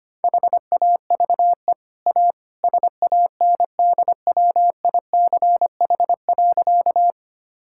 Morse code, heard as a keyed tone, about 25 wpm, 700 Hz. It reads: HAVE A SANDWICH.
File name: Easy Sentences with Hints - 25wpm